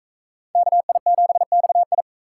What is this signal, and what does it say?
Morse code, heard as a keyed tone, about 44 wpm, 700 Hz. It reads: KI7XI